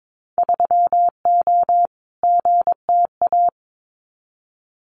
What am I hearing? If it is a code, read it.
Morse code, heard as a keyed tone, about 22 wpm, 700 Hz. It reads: 3O GTA
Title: Winter Field Day Exchanges Encore 22wpm